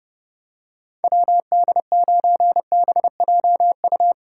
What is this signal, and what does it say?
Morse code, heard as a keyed tone, about 32 wpm, 700 Hz. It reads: WD9BJU
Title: USA Call Signs - 6 Characters - 2x Character Spacing - 30wpm